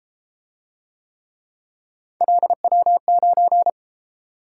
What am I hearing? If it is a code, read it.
Morse code, heard as a keyed tone, about 32 wpm, 700 Hz. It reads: LW9